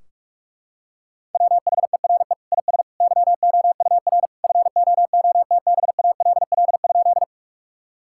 Morse code, heard as a keyed tone, about 46 wpm, 700 Hz. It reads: WHERE IS YOUR FOOTBALL?
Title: Sentences from Top 1600 Words 45wpm